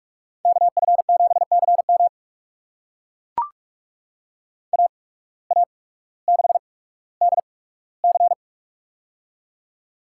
Morse code, heard as a keyed tone, about 44 wpm, spaced as for 19 wpm, 690 Hz. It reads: KF7CM AA6DC
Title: USA Call Signs - 5 Characters - 8x Character Spacing - 45wpm